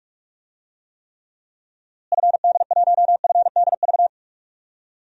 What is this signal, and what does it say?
Morse code, heard as a keyed tone, about 46 wpm, 700 Hz. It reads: FD1FDV
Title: DXCC-100 Call Signs - 6 Characters - Repeated 3x - 45wpm